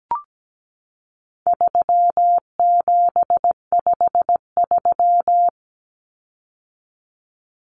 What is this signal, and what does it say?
Morse code, heard as a keyed tone, about 17 wpm, 700 Hz. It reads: 3753
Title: Common Cut Numbers - 4-Digit 17wpm